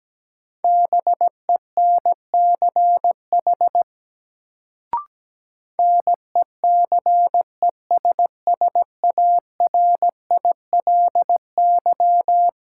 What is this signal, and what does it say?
Morse code, heard as a keyed tone, about 17 wpm, 700 Hz. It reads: BENCH NECESSARILY
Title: Sentences from Top 1400 Words - New Word Drill - 17wpm